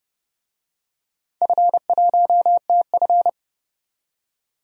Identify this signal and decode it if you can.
Morse code, heard as a keyed tone, about 30 wpm, 700 Hz. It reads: F1TF
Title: DXCC-100 Call Signs - 4 Characters - 30wpm